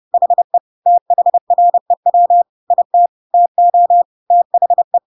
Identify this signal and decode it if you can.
Morse code, heard as a keyed tone, about 30 wpm, 700 Hz. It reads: HE THREW IT TO THE
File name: Pinocchio / Pinocchio ch1 30 30 ik1hge